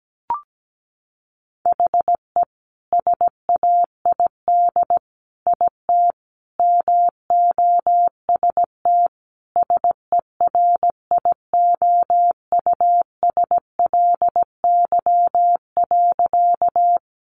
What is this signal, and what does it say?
Morse code, heard as a keyed tone, about 17 wpm, 700 Hz. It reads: HE SAID IT MOST SERIOUSLY.
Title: Sentences from Top 1200 Words 17wpm